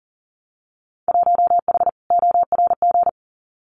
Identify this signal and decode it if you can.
Morse code, heard as a keyed tone, about 42 wpm, 700 Hz. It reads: 1H ORG
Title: Winter Field Day Exchanges 40wpm